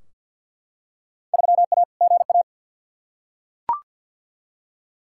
Morse code, heard as a keyed tone, about 51 wpm, 700 Hz. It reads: 3A GA
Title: Field Day Exchanges - 50wpm